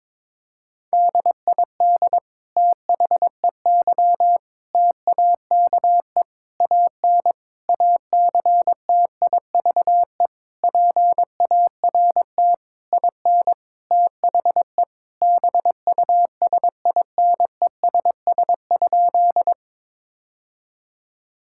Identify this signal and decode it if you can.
Morse code, heard as a keyed tone, about 21 wpm, 700 Hz. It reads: DID THEY TAKE AN ACTIVE PART IN THE BUSINESS?
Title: Sentences from Top 1000 Words - Review Part 6 - 22wpm